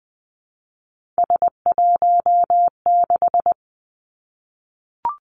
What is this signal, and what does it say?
Morse code, heard as a keyed tone, about 20 wpm, 700 Hz. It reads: S16
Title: Three Challenging Letter-Number Combinations 20wpm